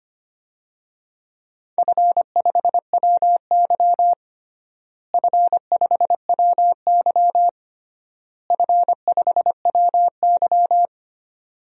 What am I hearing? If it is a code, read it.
Morse code, heard as a keyed tone, about 25 wpm, 700 Hz. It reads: F5WY F5WY F5WY
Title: DXCC-100 Call Signs - 4 Characters - Repeated 3x - 25wpm